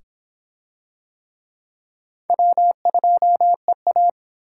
Morse code, heard as a keyed tone, about 26 wpm, 700 Hz. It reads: W2EA